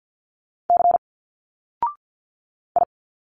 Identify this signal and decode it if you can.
Morse code, heard as a keyed tone, about 50 wpm, 700 Hz. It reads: C I
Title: Single Letters 50wpm